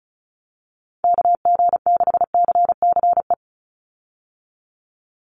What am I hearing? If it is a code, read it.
Morse code, heard as a keyed tone, about 34 wpm, 700 Hz. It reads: KG6CCE
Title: USA Call Signs 35wpm